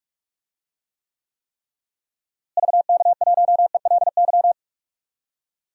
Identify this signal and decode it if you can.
Morse code, heard as a keyed tone, about 41 wpm, 700 Hz. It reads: VK1ELY
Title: DXCC-100 Call Signs - 6 Characters - Repeated 3x - 45wpm